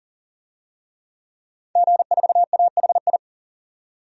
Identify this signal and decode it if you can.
Morse code, heard as a keyed tone, about 40 wpm, 700 Hz. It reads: G4AHI
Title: DXCC-100 Call Signs - 5 Characters - 40wpm